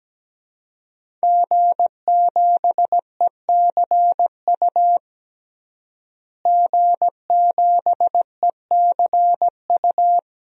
Morse code, heard as a keyed tone, about 17 wpm, 700 Hz. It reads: G7ECU G7ECU
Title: DXCC-100 Call Signs - 5 Characters - Repeated 3x - 17wpm